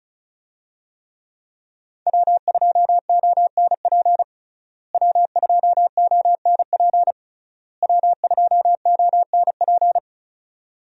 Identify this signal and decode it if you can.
Morse code, heard as a keyed tone, about 36 wpm, 700 Hz. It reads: W2ONP W2ONP W2ONP
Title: USA Call Signs - 5 Characters - Repeated 3x - 35wpm